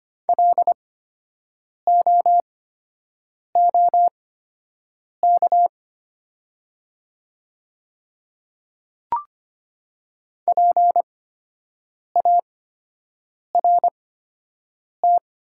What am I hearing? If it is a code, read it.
Morse code, heard as a keyed tone, about 25 wpm, spaced as for 7 wpm, 700 Hz. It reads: LOOK PART
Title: 8x Character Spacing - Top 200 Words - 25wpm